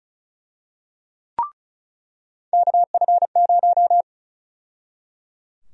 Morse code, heard as a keyed tone, about 34 wpm, 700 Hz. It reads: KF0
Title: USA Call Sign Prefixes - 35wpm